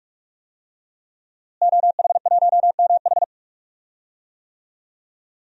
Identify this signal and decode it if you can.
Morse code, heard as a keyed tone, about 46 wpm, 700 Hz. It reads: OH1MH
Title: DXCC-100 Call Signs - 5 Characters - Repeated 2x - 45wpm